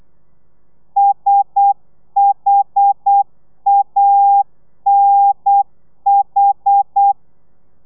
Morse code, heard as a keyed tone, about 8 wpm, 800 Hz. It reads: SHANH